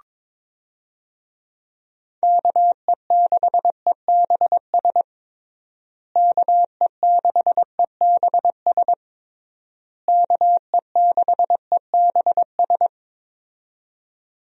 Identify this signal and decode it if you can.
Morse code, heard as a keyed tone, about 22 wpm, 700 Hz. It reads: KE6EBS KE6EBS KE6EBS
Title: USA Call Signs - 6 Characters - Repeated 3x - 22wpm